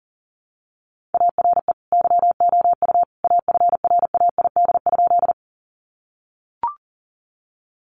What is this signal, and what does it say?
Morse code, heard as a keyed tone, about 40 wpm, 700 Hz. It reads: ARE YOU AFRAID?
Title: Sentences from Top 1500 Words 40wpm